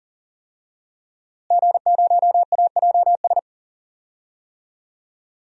Morse code, heard as a keyed tone, about 40 wpm, 700 Hz. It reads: G0AJS
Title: DXCC-100 Call Signs - 5 Characters - Repeated 3x - 40wpm